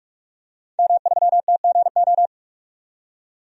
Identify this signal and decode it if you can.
Morse code, heard as a keyed tone, about 45 wpm, 700 Hz. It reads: M3TGO
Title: DXCC-100 Call Signs - 5 Characters - 45wpm